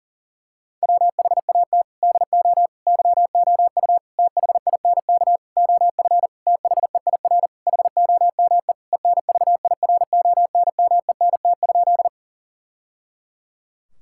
Morse code, heard as a keyed tone, about 40 wpm, 700 Hz. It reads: WHAT DO YOU THINK OF THEIR HOME ENVIRONMENT?
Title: Sentences from Top 1000 Words - Review Part 5 - 40wpm